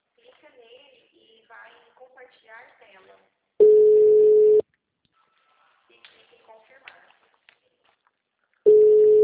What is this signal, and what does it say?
Ringback tone (Europe, 425 Hz), with speech.